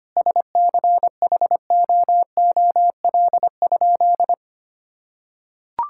Morse code, heard as a keyed tone, about 25 wpm, 700 Hz. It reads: SCHOOL?
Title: Sentences from Top 1000 Words - Review Part 6 - 25wpm